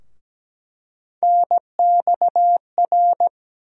Morse code, heard as a keyed tone, about 17 wpm, 700 Hz. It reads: NXR
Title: USA Call Sign Postfixes - 3 Letters 17wpm